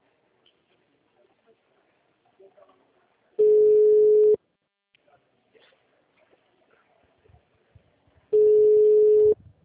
Ringback tone (Europe, 425 Hz).